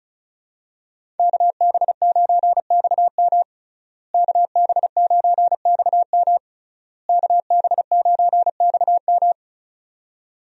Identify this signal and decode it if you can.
Morse code, heard as a keyed tone, about 36 wpm, 700 Hz. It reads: KB9XM KB9XM KB9XM
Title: USA Call Signs - 5 Characters - Repeated 3x - 35wpm